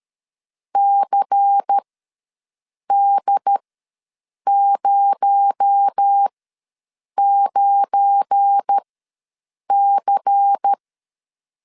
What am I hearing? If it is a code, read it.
Morse code, heard as a keyed tone, about 13 wpm, spaced as for 7 wpm, 800 Hz. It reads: CD09C
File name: LEZIONI / 8 YJ45 / 8 YJ ws 12wpm 3342